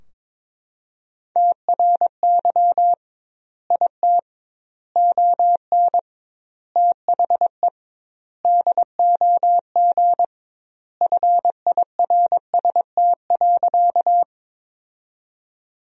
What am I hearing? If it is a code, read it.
Morse code, heard as a keyed tone, about 22 wpm, 700 Hz. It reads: TRY IT ON THE DOG FIRST.
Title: 2x Word-Spacing - Sentences from Top 400 Words - 22wpm